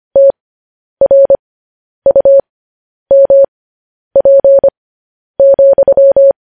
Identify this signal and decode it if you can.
Morse code, heard as a keyed tone, about 25 wpm, spaced as for 10 wpm, 550 Hz. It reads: TRUMP,
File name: podcast / News headlines-1x-25x10